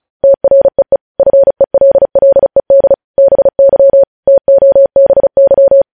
Morse code, heard as a keyed tone, about 36 wpm, 550 Hz. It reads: TREE FELLED BY TOBY
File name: podcast / News headlines-1x-35x35